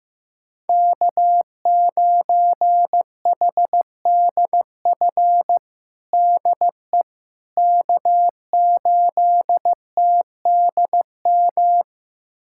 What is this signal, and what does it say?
Morse code, heard as a keyed tone, about 15 wpm, 700 Hz. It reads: K9HDF DE K8TDM
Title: USA Callsign DE Callsign - 5 Characters - 15wpm